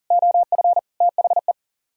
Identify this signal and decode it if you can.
Morse code, heard as a keyed tone, about 52 wpm, 700 Hz. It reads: OF THE